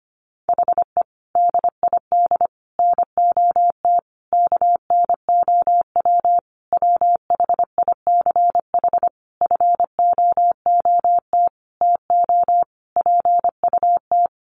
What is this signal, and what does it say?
Morse code, heard as a keyed tone, about 25 wpm, 700 Hz. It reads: HE DID NOT KNOW WHICH FOOT TO PUT
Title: Sentences from Top 1300 Words 25wpm